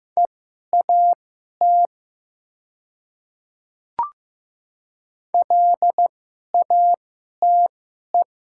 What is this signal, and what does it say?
Morse code, heard as a keyed tone, about 15 wpm, spaced as for 11 wpm, 700 Hz. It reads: EAT LATE
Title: 2x Character Spacing - Top 300 Words - 15wpm